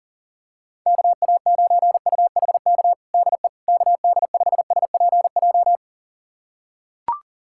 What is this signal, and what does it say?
Morse code, heard as a keyed tone, about 37 wpm, 700 Hz. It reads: KA9UHK DE KD5SPJ
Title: USA Callsign DE Callsign - 6 Characters - 40wpm